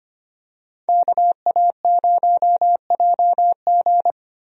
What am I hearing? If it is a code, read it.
Morse code, heard as a keyed tone, about 25 wpm, 700 Hz. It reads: KA0JG
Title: USA Call Signs - 5 Characters - Repeated 3x - 25wpm